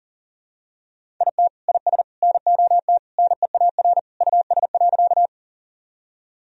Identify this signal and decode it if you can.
Morse code, heard as a keyed tone, about 37 wpm, 700 Hz. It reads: IT IS NOT NEAR US.
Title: Sentences from Top 1000 Words - Review Part 1 - 40wpm